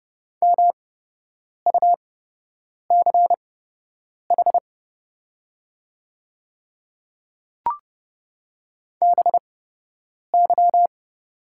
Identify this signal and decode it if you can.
Morse code, heard as a keyed tone, about 31 wpm, spaced as for 8 wpm, 700 Hz. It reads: MUCH BY
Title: 8x Character Spacing - Top 200 Words - 30wpm